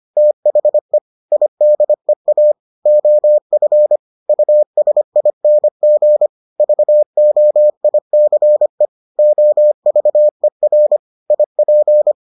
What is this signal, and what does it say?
Morse code, heard as a keyed tone, about 25 wpm, 600 Hz. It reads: THE IDEA OF USING VOICE OVER IP